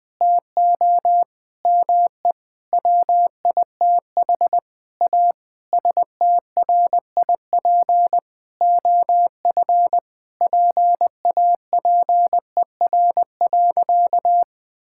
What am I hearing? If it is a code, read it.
Morse code, heard as a keyed tone, about 21 wpm, 700 Hz. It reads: TO ME WITH A STRIP OF PAPER.
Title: Sentences from Top 1600 Words 20wpm